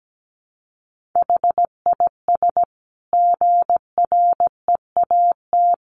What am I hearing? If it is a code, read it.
Morse code, heard as a keyed tone, about 17 wpm, 700 Hz. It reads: HIS GREAT
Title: Sentences from Top 1400 Words 17wpm